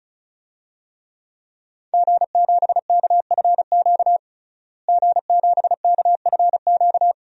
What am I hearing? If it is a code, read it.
Morse code, heard as a keyed tone, about 36 wpm, 700 Hz. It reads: G7KFQ G7KFQ
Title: DXCC-100 Call Signs - 5 Characters - Repeated 2x - 35wpm